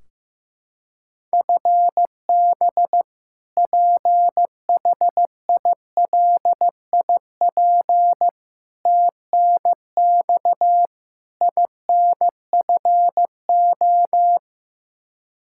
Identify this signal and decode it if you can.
Morse code, heard as a keyed tone, about 15 wpm, 700 Hz. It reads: FB PHILIP TNX INFO
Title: Practice QSO - Exchange 3 - More Info - 15wpm